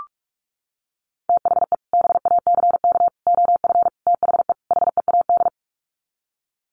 Morse code, heard as a keyed tone, about 44 wpm, 700 Hz. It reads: THE BACK OF THE HEAD
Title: Sets of 5 Words Encore2 45wpm